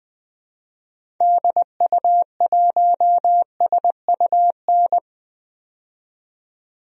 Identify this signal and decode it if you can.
Morse code, heard as a keyed tone, about 20 wpm, 700 Hz. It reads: DU1SUN
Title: DXCC-100 Call Signs - 6 Characters - 20wpm